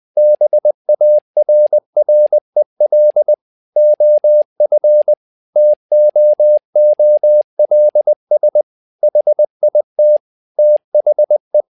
Morse code, heard as a keyed tone, about 20 wpm, 600 Hz. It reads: BARREL OF TOOLS HIT THE
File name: learning center / tower trouble / tower trouble 20wpm